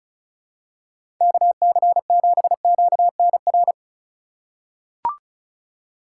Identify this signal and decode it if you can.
Morse code, heard as a keyed tone, about 34 wpm, 700 Hz. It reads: KC7QNR
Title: USA Call Signs - 6 Characters - 4x Character Spacing - 35wpm